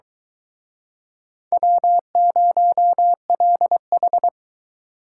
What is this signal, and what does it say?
Morse code, heard as a keyed tone, about 24 wpm, 700 Hz. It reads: W0LH